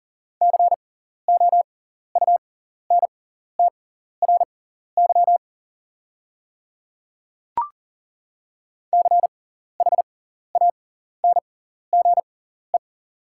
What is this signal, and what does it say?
Morse code, heard as a keyed tone, about 40 wpm, spaced as for 14 wpm, 690 Hz. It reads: COUNTRY CHANGE
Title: 6x Character Spacing - Top 300 Words - 40wpm